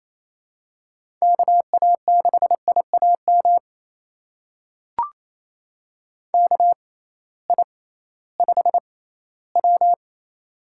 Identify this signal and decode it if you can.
Morse code, heard as a keyed tone, about 28 wpm, spaced as for 16 wpm, 700 Hz. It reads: KA6IAM KI5W
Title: USA Call Signs - 6x Character Spacing - 28wpm